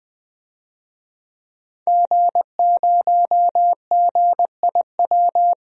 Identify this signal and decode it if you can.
Morse code, heard as a keyed tone, about 21 wpm, 700 Hz. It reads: G0GIW
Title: DXCC-100 Call Signs - 5 Characters - Repeated 3x - 20wpm